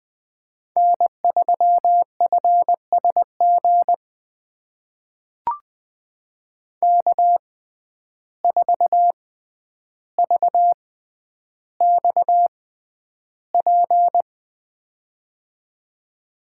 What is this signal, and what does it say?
Morse code, heard as a keyed tone, about 20 wpm, spaced as for 10 wpm, 700 Hz. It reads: N3FSG K4VXP
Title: USA Call Signs - 5 Characters - 6x Character Spacing - 20wpm